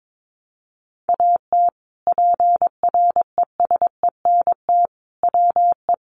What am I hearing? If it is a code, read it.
Morse code, heard as a keyed tone, about 22 wpm, 700 Hz. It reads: AT PRESENT WE
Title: Sentences from Top 800 Words 22wpm